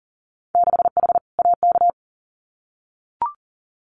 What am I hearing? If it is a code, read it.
Morse code, heard as a keyed tone, about 38 wpm, 700 Hz. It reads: 6H AK
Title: Winter Field Day Exchanges Encore 40wpm